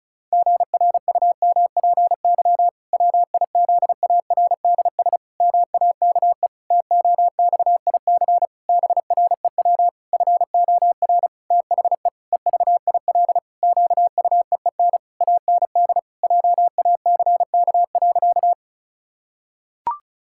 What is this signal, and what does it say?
Morse code, heard as a keyed tone, about 36 wpm, 700 Hz. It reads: GRUMPY WIZARDS MAKE TOXIC BREW FOR THE EVIL QUEEN AND JACK.
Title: Pangram Challenge v2 - 35wpm